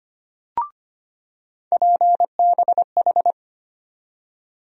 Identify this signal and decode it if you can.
Morse code, heard as a keyed tone, about 25 wpm, 700 Hz. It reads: PBH
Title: USA Call Sign Postfixes - 3 Letters 25wpm